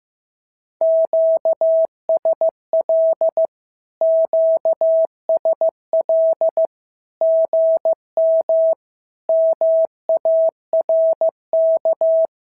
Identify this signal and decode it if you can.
Morse code, heard as a keyed tone, about 15 wpm, 650 Hz. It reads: QSL QSL GM MARK
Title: SOTA QSO Practice 15wpm